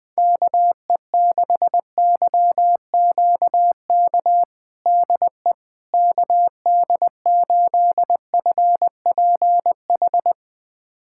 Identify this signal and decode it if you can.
Morse code, heard as a keyed tone, about 21 wpm, 700 Hz. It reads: KE6YQK DE KD8FPH
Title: USA Callsign DE Callsign 20wpm